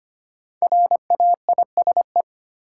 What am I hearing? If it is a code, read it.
Morse code, heard as a keyed tone, about 25 wpm, 700 Hz. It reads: RAISE
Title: Sentences from Top 1400 Words - New Word Drill Alt - 25wpm